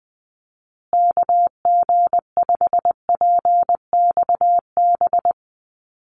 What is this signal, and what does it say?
Morse code, heard as a keyed tone, about 20 wpm, 700 Hz. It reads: KG5PXB
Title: USA Call Signs Repeated 2x 20wpm